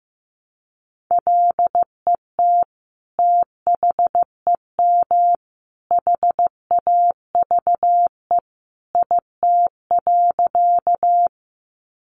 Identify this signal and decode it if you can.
Morse code, heard as a keyed tone, about 15 wpm, 700 Hz. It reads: LET THEM HAVE IT.
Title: Sentences from Top 300 Words 15wpm